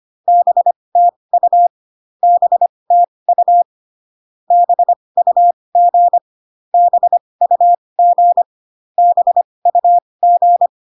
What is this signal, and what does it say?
Morse code, heard as a keyed tone, about 25 wpm, spaced as for 20 wpm, 700 Hz. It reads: BTU BTU BUG BUG BUG